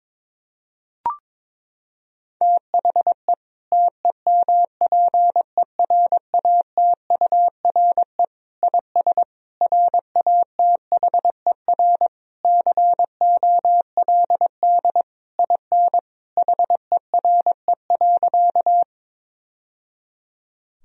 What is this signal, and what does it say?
Morse code, heard as a keyed tone, about 22 wpm, 700 Hz. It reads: THE TEMPERATURE IS RATHER COLD IN HERE.
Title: Sentences from Top 1000 Words - Review Part 4 - 22wpm